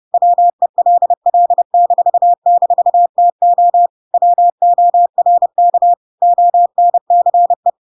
Morse code, heard as a keyed tone, about 30 wpm, 700 Hz. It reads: WELL--TO WORK ONCE
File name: Pinocchio / Pinocchio ch1 30 30 ik1hge